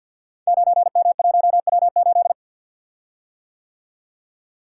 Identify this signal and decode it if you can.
Morse code, heard as a keyed tone, about 51 wpm, 700 Hz. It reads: 9M1W8
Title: Code Groups - 5 Letter:Numbers - 50wpm